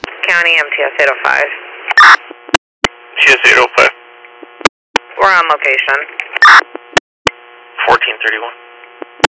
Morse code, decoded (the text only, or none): none